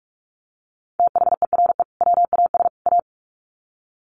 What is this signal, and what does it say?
Morse code, heard as a keyed tone, about 44 wpm, 700 Hz. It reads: THERE WAS A
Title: Sets of 3 Words 45wpm